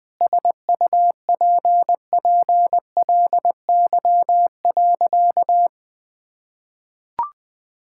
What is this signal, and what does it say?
Morse code, heard as a keyed tone, about 21 wpm, 700 Hz. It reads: SUPPLY.
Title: Sentences from Top 1000 Words - Review Part 5 - 20wpm